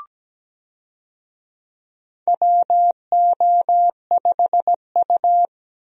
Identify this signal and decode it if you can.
Morse code, heard as a keyed tone, about 17 wpm, 700 Hz. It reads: WO5U
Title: USA Call Signs - 4 Characters - 17wpm